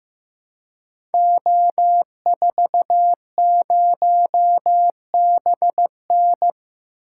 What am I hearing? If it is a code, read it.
Morse code, heard as a keyed tone, about 15 wpm, 700 Hz. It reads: O40BN
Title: Code Groups - 5 Letter:Numbers - 15wpm